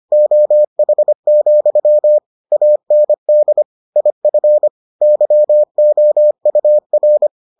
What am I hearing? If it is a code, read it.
Morse code, heard as a keyed tone, about 25 wpm, 600 Hz. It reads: OH, AND IF YOUR